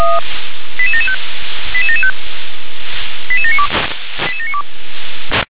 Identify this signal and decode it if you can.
Five-tone selective call (ZVEI), heard as quick runs of five tones, tones of about 80 ms. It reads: 88775 88775 88792 88792